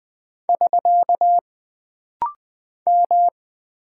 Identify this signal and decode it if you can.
Morse code, heard as a keyed tone, about 20 wpm, 700 Hz. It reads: <SK> M
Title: Single Character (Letter:Number:Punctuation:Prosign) - Rapid-Fire - 20wpm